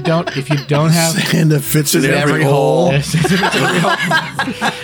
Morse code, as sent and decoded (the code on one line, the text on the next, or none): none